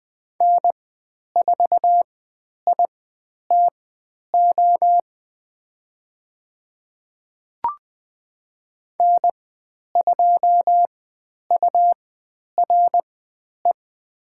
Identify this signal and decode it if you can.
Morse code, heard as a keyed tone, about 20 wpm, spaced as for 10 wpm, 700 Hz. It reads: N4ITO N2URE
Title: 109 Lesson 18g - Letter m - Call Signs 20wpm